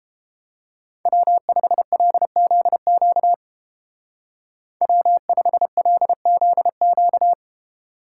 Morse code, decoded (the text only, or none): W5LZQ W5LZQ